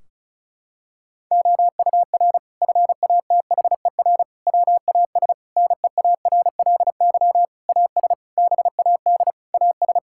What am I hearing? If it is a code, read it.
Morse code, heard as a keyed tone, about 36 wpm, 700 Hz. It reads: OUR FATHER WAS NEARLY AS BAD AS
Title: Sentences from Top 1600 Words 35wpm